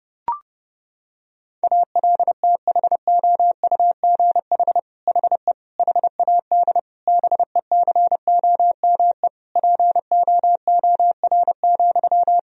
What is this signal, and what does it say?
Morse code, heard as a keyed tone, about 32 wpm, 700 Hz. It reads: ALTHOUGH HE HAD BECOME POOR,
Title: Sentences from Top 1000 Words - Review Part 3 - 30wpm